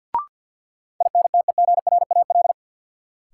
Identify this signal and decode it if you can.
Morse code, heard as a keyed tone, about 52 wpm, 700 Hz. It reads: INTEGRAL
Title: Sets of 5 Words - New Word Drill - 50wpm